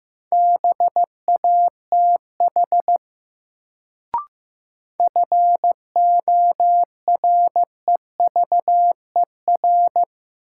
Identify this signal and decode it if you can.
Morse code, heard as a keyed tone, about 15 wpm, 700 Hz. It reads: BATH FOREVER
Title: Sentences from Top 1600 Words - New Word Drill - 15wpm